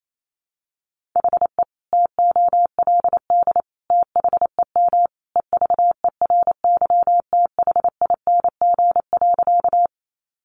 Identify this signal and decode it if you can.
Morse code, heard as a keyed tone, about 28 wpm, 700 Hz. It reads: HE TOLD THEM EVERYTHING.
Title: Sentences from Top 700 Words 28wpm